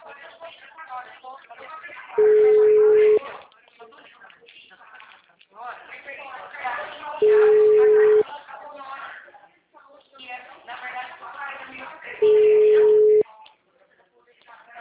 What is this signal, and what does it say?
Ringback tone (Europe, 425 Hz), with speech.
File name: out- -2001- - - .39555